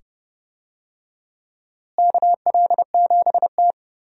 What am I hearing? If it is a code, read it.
Morse code, heard as a keyed tone, about 30 wpm, 700 Hz. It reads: KL7T
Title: DXCC-100 Call Signs - 4 Characters - Repeated 2x - 30wpm